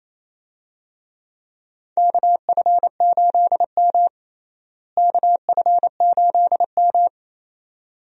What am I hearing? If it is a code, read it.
Morse code, heard as a keyed tone, about 28 wpm, 700 Hz. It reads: KF8M KF8M
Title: USA Call Signs - 4 Characters - Repeated 2x - 28wpm